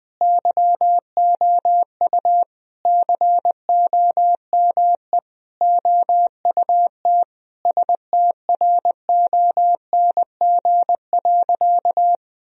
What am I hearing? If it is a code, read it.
Morse code, heard as a keyed tone, about 20 wpm, 700 Hz. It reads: YOU COME OUT STRONG.
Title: Sentences from Top 1000 Words - Review Part 3 - 20wpm